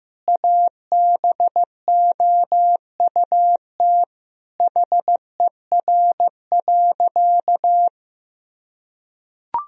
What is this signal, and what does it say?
Morse code, heard as a keyed tone, about 15 wpm, 700 Hz. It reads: ABOUT HER.
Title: Sentences from Top 1000 Words - Review Part 4 - 15wpm